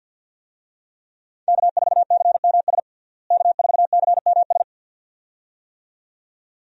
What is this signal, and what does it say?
Morse code, heard as a keyed tone, about 52 wpm, 700 Hz. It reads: K4CMS K4CMS
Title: USA Call Signs - 5 Characters - Repeated 2x - 50wpm